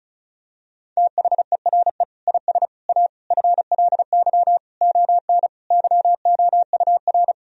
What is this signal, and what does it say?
Morse code, heard as a keyed tone, about 34 wpm, 700 Hz. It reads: THERE IS A FLY ON YOUR
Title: Sentences from Top 1000 Words - Review Part 2 - 35wpm